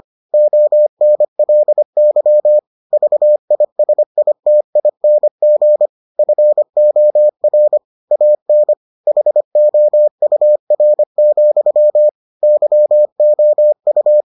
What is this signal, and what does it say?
Morse code, heard as a keyed tone, about 25 wpm, 600 Hz. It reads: ONLY VISITING FOR AN HOUR, YOU